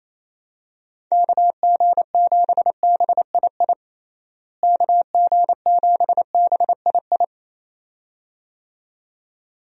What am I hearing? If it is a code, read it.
Morse code, heard as a keyed tone, about 28 wpm, 700 Hz. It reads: KG7BII KG7BII
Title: USA Call Signs - 6 Characters - Repeated 2x - 28wpm